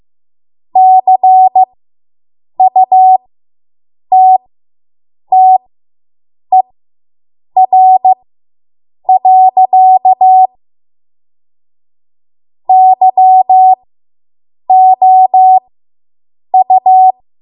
Morse code, heard as a keyed tone, about 15 wpm, spaced as for 7 wpm, 750 Hz. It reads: CUTTER. YOU